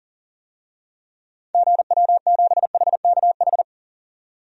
Morse code, heard as a keyed tone, about 37 wpm, 700 Hz. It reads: GW7HKH
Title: DXCC-100 Call Signs - 6 Characters - 40wpm